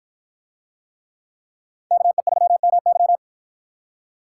Morse code, heard as a keyed tone, about 53 wpm, 700 Hz. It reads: KE3MY